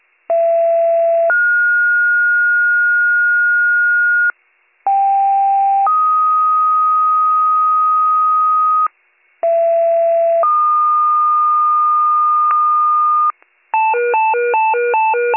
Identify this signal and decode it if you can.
Two-tone page, heard as a one-second tone, then a three-second tone, about 670 then 1430 Hz, about 790 then 1250 Hz, about 670 then 1220 Hz, with noise. It is heard from 0.3 to 13.3 s.